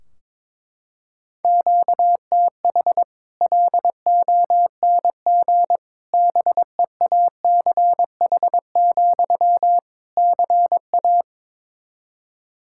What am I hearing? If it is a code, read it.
Morse code, heard as a keyed tone, about 22 wpm, 700 Hz. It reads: QTH LONG BEACH, CA
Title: Practice QSO - Exchange 1 - Beginning - 22wpm